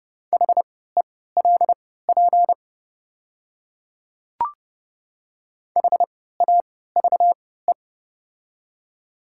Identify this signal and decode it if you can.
Morse code, heard as a keyed tone, about 30 wpm, spaced as for 17 wpm, 690 Hz. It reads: HELP HAVE
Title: 3x Character Spacing - Top 200 Words - 30wpm